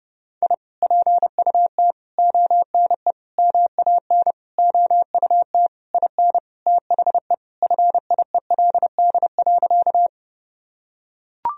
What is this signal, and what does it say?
Morse code, heard as a keyed tone, about 32 wpm, 700 Hz. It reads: I PUT ONE MAN OUT IN THE FIELD.
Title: Sentences from Top 1000 Words - Review Part 2 - 30wpm